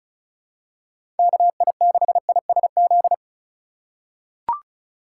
Morse code, heard as a keyed tone, about 34 wpm, 700 Hz. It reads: KI6ISZ
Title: USA Call Signs - 3x Character Spacing - 35wpm